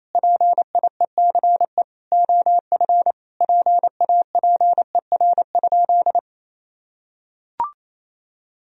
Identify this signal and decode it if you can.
Morse code, heard as a keyed tone, about 28 wpm, 700 Hz. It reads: PIECE OF PAPER?
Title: Sentences from Top 1000 Words - Review Part 2 - 28wpm